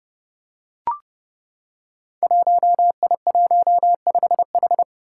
Morse code, heard as a keyed tone, about 30 wpm, 700 Hz. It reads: 1I15H
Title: Five Challenging Letter-Number Combinations 30wpm